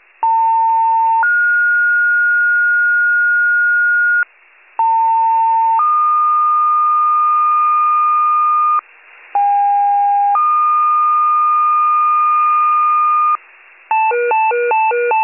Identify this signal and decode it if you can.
Two-tone page, heard as a one-second tone, then a three-second tone, about 900 then 1430 Hz, about 900 then 1220 Hz, about 810 then 1220 Hz. It is heard from 0.2 to 13.4 s.